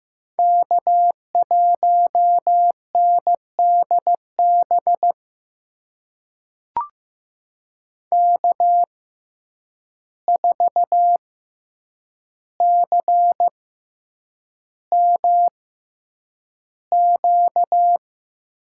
Morse code, heard as a keyed tone, about 15 wpm, spaced as for 8 wpm, 700 Hz. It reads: K1NDB K4CMQ
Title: USA Call Signs - 5 Characters - 6x Character Spacing - 15wpm